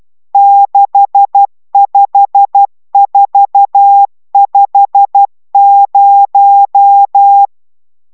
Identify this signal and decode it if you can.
Morse code, heard as a keyed tone, about 12 wpm, 800 Hz. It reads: 65450